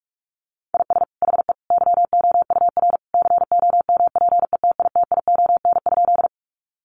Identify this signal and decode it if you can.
Morse code, heard as a keyed tone, about 44 wpm, 700 Hz. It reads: IS HE YOUR COMPETITION?
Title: Sentences from Top 1000 Words 45wpm